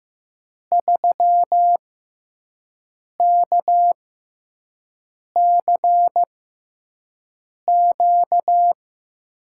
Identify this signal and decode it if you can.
Morse code, heard as a keyed tone, about 16 wpm, spaced as for 5 wpm, 700 Hz. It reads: 3KCQ
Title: USA Call Signs - 5 Characters - 6x Character Spacing - 15wpm